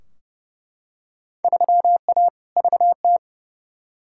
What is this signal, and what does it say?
Morse code, heard as a keyed tone, about 30 wpm, 700 Hz. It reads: 3A VT
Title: Field Day Exchanges Encore - 30wpm